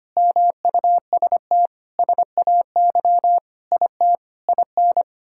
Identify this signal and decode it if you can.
Morse code, heard as a keyed tone, about 25 wpm, 700 Hz. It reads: MUST SAY IT IN
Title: Sentences from Top 1600 Words 25wpm